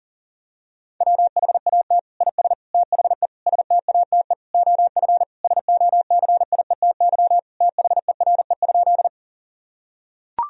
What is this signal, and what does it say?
Morse code, heard as a keyed tone, about 40 wpm, 700 Hz. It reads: WHAT IS THE STATE OF SOCIETY THERE?
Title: Sentences from Top 1000 Words - Review Part 4 - 40wpm